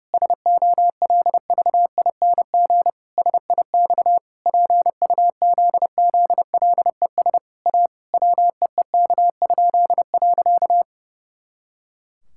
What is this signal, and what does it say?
Morse code, heard as a keyed tone, about 30 wpm, 700 Hz. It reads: SOLVING SIX PUZZLES A WEEK?.
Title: Pangram Challenge v2 - 30wpm